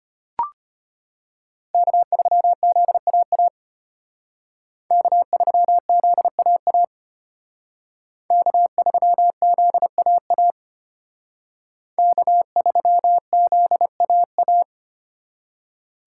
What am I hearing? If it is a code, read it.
Morse code, heard as a keyed tone, about 33 wpm, 700 Hz. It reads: K3ZAA K3ZAA K3ZAA K3ZAA